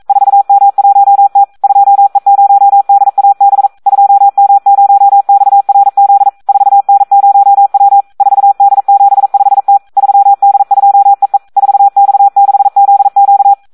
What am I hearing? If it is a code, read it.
Morse code, heard as a keyed tone, about 42 wpm, 800 Hz. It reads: 4M1T 2E0DAB 2M0XRZ 4N0W 4D75T 3D2EE 4X6ZQ
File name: Audio practica CW / SD 1 42